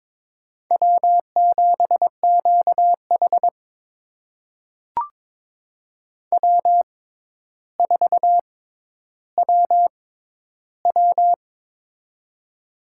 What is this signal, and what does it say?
Morse code, heard as a keyed tone, about 22 wpm, spaced as for 11 wpm, 700 Hz. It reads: W7QH W4WW
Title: USA Call Signs - 4 Characters - 6x Character Spacing - 22wpm